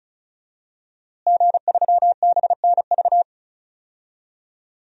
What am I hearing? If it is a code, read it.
Morse code, heard as a keyed tone, about 34 wpm, 700 Hz. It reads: G3BNV
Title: DXCC-100 Call Signs - 5 Characters - Repeated 3x - 35wpm